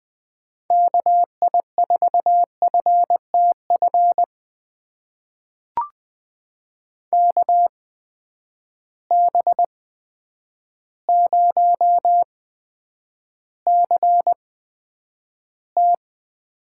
Morse code, heard as a keyed tone, about 20 wpm, spaced as for 9 wpm, 700 Hz. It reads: KI4FTF KB0CT
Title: USA Call Signs - 6 Characters - 8x Character Spacing - 20wpm